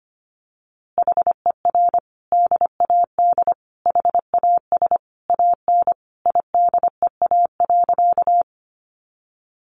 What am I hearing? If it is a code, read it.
Morse code, heard as a keyed tone, about 25 wpm, 700 Hz. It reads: HER DAD HAS AN IDEA.
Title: Easy Phrases 25wpm